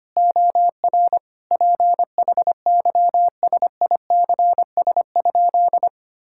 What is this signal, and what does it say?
Morse code, heard as a keyed tone, about 25 wpm, 700 Hz. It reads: OR PHYSICS?
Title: Sentences from Top 1000 Words - Review Part 6 - 25wpm